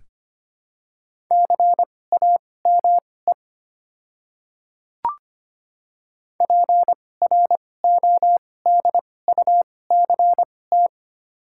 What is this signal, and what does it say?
Morse code, heard as a keyed tone, about 25 wpm, spaced as for 18 wpm, 700 Hz. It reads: CAME PRODUCT
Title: 2x Character Spacing - Top 500 Words - 25wpm